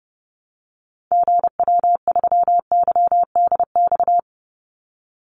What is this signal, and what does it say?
Morse code, heard as a keyed tone, about 30 wpm, 700 Hz. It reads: GW3YDX
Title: Call Signs - 30wpm